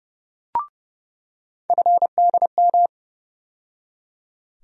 Morse code, heard as a keyed tone, about 30 wpm, 700 Hz. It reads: FDM
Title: USA Call Sign Postfixes - 3 Letters 30wpm